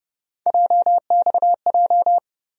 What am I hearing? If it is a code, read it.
Morse code, heard as a keyed tone, about 30 wpm, 700 Hz. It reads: JXJ
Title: USA Call Sign Postfixes - 3 Letters 30wpm